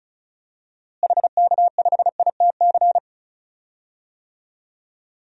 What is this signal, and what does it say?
Morse code, heard as a keyed tone, about 36 wpm, 700 Hz. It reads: HK5ITC
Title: DXCC-100 Call Signs - 6 Characters - Repeated 2x - 35wpm